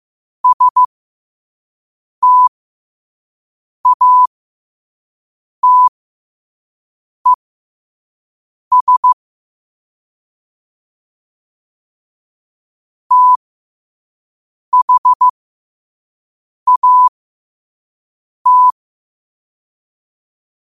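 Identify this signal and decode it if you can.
Morse code, heard as a keyed tone, about 15 wpm, spaced as for 5 wpm, 1000 Hz. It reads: STATES THAT
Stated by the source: Morse at 5 wpm.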